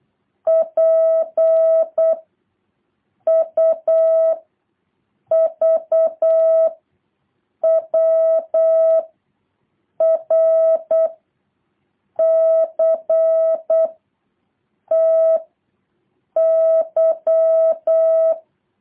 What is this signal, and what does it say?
Morse code, heard as a keyed tone, about 8 wpm, 650 Hz. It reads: P U V W R C T Y